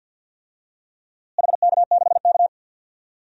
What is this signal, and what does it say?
Morse code, heard as a keyed tone, about 51 wpm, 700 Hz. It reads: HK6K
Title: DXCC-100 Call Signs - 4 Characters - Repeated 2x - 50wpm